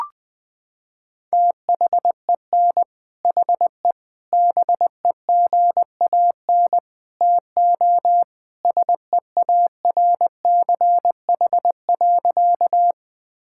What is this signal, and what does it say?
Morse code, heard as a keyed tone, about 21 wpm, 700 Hz. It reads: THEN HE BEGAN TO SEARCH.
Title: Sentences from Top 1000 Words - Review Part 5 - 20wpm